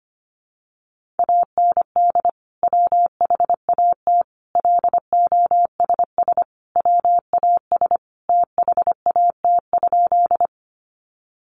Morse code, heard as a keyed tone, about 25 wpm, 700 Hz. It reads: AND WHAT LOSS WAS THAT?
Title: Sentences from Top 1400 Words 25wpm